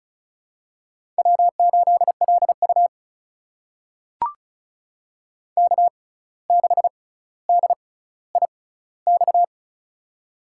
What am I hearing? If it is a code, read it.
Morse code, heard as a keyed tone, about 36 wpm, spaced as for 17 wpm, 700 Hz. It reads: W8LU K6DIX
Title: USA Call Signs - 6x Character Spacing - 35wpm